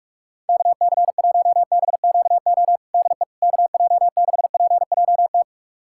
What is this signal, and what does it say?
Morse code, heard as a keyed tone, about 42 wpm, 700 Hz. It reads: KC1BQO DE KJ6PJT
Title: USA Callsign DE Callsign - 6 Characters - 45wpm